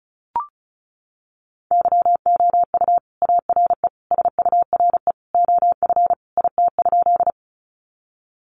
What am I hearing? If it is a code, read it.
Morse code, heard as a keyed tone, about 34 wpm, 700 Hz. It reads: YOU ARE SURE OF IT?
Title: Sentences from Top 300 Words 35wpm